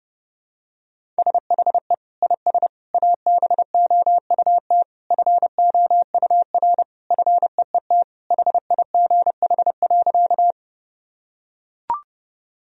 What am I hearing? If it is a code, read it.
Morse code, heard as a keyed tone, about 30 wpm, 700 Hz. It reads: SHE IS ABOUT FOUR FEET HIGH.
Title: Sentences from Top 1000 Words - Review Part 1 - 30wpm